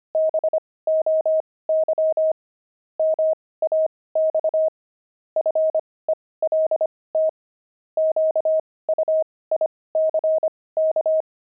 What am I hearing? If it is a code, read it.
Morse code, heard as a keyed tone, about 25 wpm, spaced as for 18 wpm, 630 Hz. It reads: BOY MAX FELT QUICK
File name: pangram-fs-18-cs-25